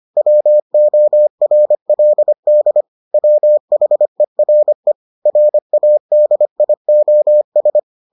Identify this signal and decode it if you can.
Morse code, heard as a keyed tone, about 25 wpm, 600 Hz. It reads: WORLD WHERE RADIOS